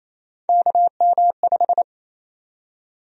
Morse code, heard as a keyed tone, about 28 wpm, 700 Hz. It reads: KM5
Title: USA Call Sign Prefixes - 28wpm